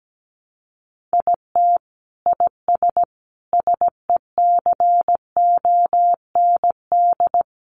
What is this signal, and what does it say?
Morse code, heard as a keyed tone, about 17 wpm, 700 Hz. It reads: IT IS SECOND
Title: Sentences from Top 600 Words 17wpm